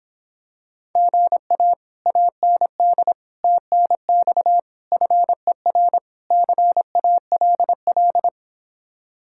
Morse code, heard as a keyed tone, about 26 wpm, 700 Hz. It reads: GA AND TNX FER CALL